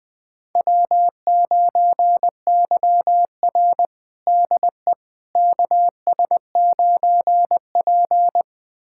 Morse code, heard as a keyed tone, about 20 wpm, 700 Hz. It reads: W9YR DE KS9P
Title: USA Callsign DE Callsign - 4 Characters - 20wpm